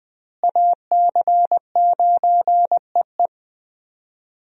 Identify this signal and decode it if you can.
Morse code, heard as a keyed tone, about 21 wpm, 700 Hz. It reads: AC9EE
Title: USA Call Signs - 5 Characters - Repeated 3x - 20wpm